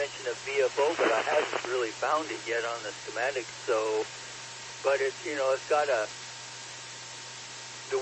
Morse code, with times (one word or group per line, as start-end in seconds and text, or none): none